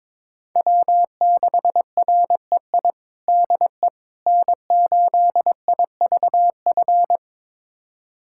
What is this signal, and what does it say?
Morse code, heard as a keyed tone, about 22 wpm, 700 Hz. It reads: W6REI DE N8IVF
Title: USA Callsign DE Callsign 22wpm